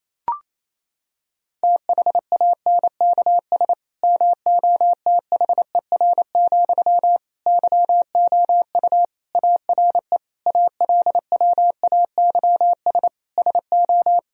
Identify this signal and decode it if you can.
Morse code, heard as a keyed tone, about 28 wpm, 700 Hz. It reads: THANKS MOTHER, YOU ARE ALWAYS SO
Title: Sentences from Top 1000 Words - Review Part 3 - 28wpm